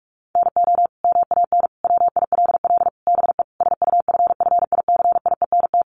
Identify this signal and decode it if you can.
Morse code, heard as a keyed tone, about 44 wpm, 700 Hz. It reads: NO MAN WILL BE SUFFICIENT
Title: Sentences from Top 1400 Words 45wpm